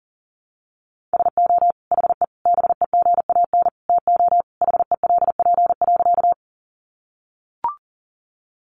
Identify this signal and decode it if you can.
Morse code, heard as a keyed tone, about 39 wpm, 700 Hz. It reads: SO HE BEGAN TO HELP.
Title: Sentences from Top 300 Words 40wpm